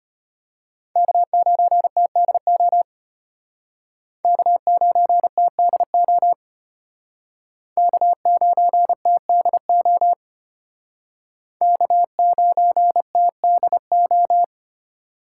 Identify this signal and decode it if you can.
Morse code, heard as a keyed tone, about 33 wpm, 700 Hz. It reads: K9TDO K9TDO K9TDO K9TDO